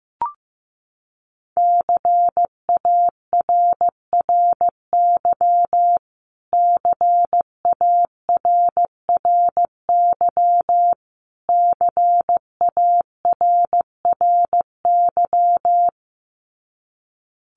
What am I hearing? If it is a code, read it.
Morse code, heard as a keyed tone, about 15 wpm, 700 Hz. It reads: CARRY CARRY CARRY
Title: Top 500 Words Repeated 15wpm